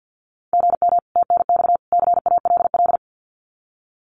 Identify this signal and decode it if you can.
Morse code, heard as a keyed tone, about 50 wpm, 700 Hz. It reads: GM TNX CALL
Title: Common QSO Phrases 50wpm